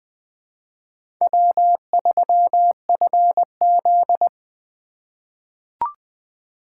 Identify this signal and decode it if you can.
Morse code, heard as a keyed tone, about 20 wpm, 700 Hz. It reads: W3FZ
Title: USA Call Signs - 3x Character Spacing - 20wpm